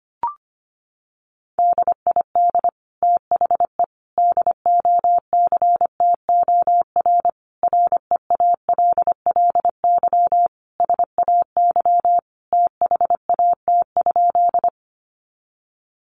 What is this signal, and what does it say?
Morse code, heard as a keyed tone, about 25 wpm, 700 Hz. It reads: DID THE DOCTOR REALLY SAY THAT?
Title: Sentences from Top 1500 Words 25wpm